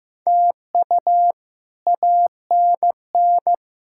Morse code, heard as a keyed tone, about 15 wpm, 700 Hz. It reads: TU ANN
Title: K1USN SST QSO Practice - 15wpm